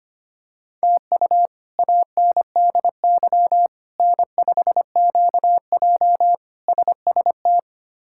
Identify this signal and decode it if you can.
Morse code, heard as a keyed tone, about 25 wpm, 700 Hz. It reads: TU ANDY N5QJ SST
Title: K1USN SST QSO Practice - 25wpm